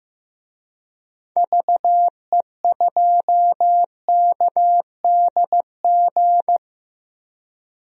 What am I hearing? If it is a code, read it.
Morse code, heard as a keyed tone, about 15 wpm, 700 Hz. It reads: VE2KDG
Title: DXCC-100 Call Signs - 6 Characters - 15wpm